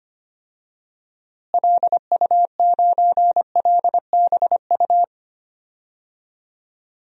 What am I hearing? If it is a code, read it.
Morse code, heard as a keyed tone, about 25 wpm, 700 Hz. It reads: LU9LBU
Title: DXCC-100 Call Signs - 6 Characters - Repeated 2x - 25wpm